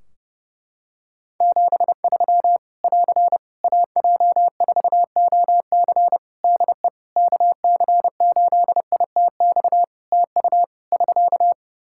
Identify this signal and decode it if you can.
Morse code, heard as a keyed tone, about 32 wpm, 700 Hz. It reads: 73 <AR> AJ4OC DE KC8ITX TU <SK>
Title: Practice QSO - Exchange 4 - Ending - 30wpm